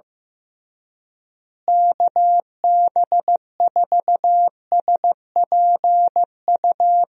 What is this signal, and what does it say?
Morse code, heard as a keyed tone, about 15 wpm, 700 Hz. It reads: KB4SPU
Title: USA Call Signs - 6 Characters - Repeated 3x - 15wpm